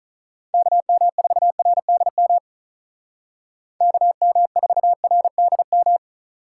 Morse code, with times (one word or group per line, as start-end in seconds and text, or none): 0.5-2.4 KM4RDM
3.8-6.0 KM4RDM